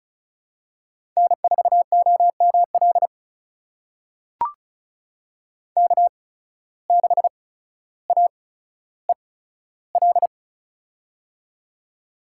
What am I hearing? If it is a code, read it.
Morse code, heard as a keyed tone, about 36 wpm, spaced as for 15 wpm, 700 Hz. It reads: N4OML K6AEL
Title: USA Call Signs - 5 Characters - 8x Character Spacing - 35wpm